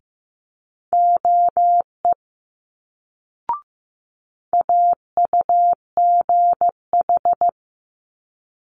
Morse code, heard as a keyed tone, about 15 wpm, 700 Hz. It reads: OE AUGH
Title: Multi-letter Phonemes Quick Fire 15wpm